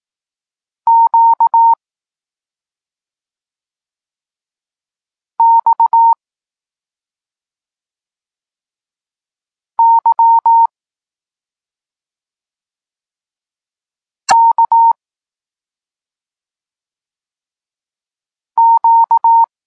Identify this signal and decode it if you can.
Morse code, heard as a keyed tone, about 18 wpm, 920 Hz. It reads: Q X Y K Q
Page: Tekst 32 – Doskonalenie odbioru poznanych liter